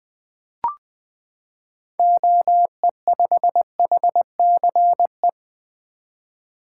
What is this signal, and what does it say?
Morse code, heard as a keyed tone, about 20 wpm, 700 Hz. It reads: OE5HCE
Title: 2023 SOTA Activator Call Signs - 20wpm